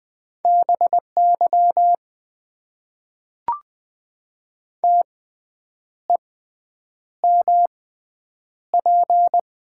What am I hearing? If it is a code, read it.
Morse code, heard as a keyed tone, about 20 wpm, spaced as for 8 wpm, 700 Hz. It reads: BY TEMP
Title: Top 90 QSO Elements - 6x Character Spacing - 20wpm